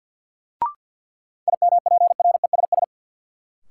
Morse code, heard as a keyed tone, about 48 wpm, 700 Hz. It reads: IMPRESS
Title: Sentences from Top 1600 Words - New Word Drill Alt - 50wpm